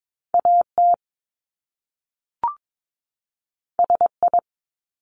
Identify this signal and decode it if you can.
Morse code, heard as a keyed tone, about 22 wpm, 700 Hz. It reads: AT SI
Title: Top Two Letter Combinations in Words 22wpm